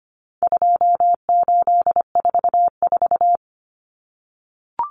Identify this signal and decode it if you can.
Morse code, heard as a keyed tone, about 25 wpm, 700 Hz. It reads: 2844
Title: Numbers - 4-Digit 25wpm